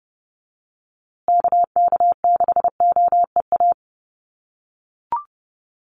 Morse code, heard as a keyed tone, about 29 wpm, 700 Hz. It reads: KK6OEA
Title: USA Call Signs Repeated 2x 30wpm